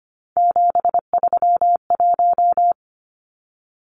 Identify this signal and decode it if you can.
Morse code, heard as a keyed tone, about 25 wpm, 700 Hz. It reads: 731
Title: Numbers - 3-Digit 25wpm